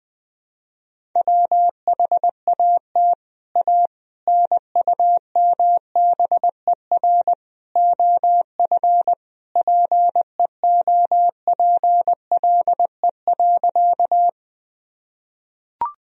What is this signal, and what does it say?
Morse code, heard as a keyed tone, about 20 wpm, 700 Hz. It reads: WHAT A NUMBER OF PEOPLE.
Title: Sentences from Top 1000 Words - Review Part 1 - 20wpm